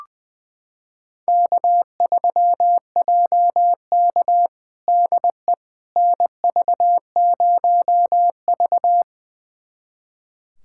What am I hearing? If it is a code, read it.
Morse code, heard as a keyed tone, about 20 wpm, 700 Hz. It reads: K3JK DE NV0V
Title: USA Callsign DE Callsign - 4 Characters - 20wpm